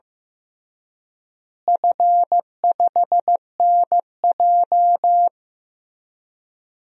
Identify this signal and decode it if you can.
Morse code, heard as a keyed tone, about 15 wpm, 700 Hz. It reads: F5NJ
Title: DXCC-100 Call Signs - 4 Characters - Repeated 3x - 15wpm